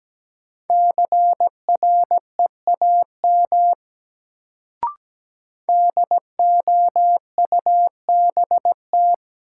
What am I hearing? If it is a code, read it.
Morse code, heard as a keyed tone, about 17 wpm, 700 Hz. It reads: CREAM DOUBT
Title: Sentences from Top 1500 Words - New Word Drill - 17wpm